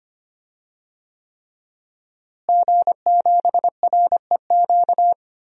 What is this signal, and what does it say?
Morse code, heard as a keyed tone, about 25 wpm, 700 Hz. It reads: G7REQ
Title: DXCC-100 Call Signs - 5 Characters - Repeated 3x - 25wpm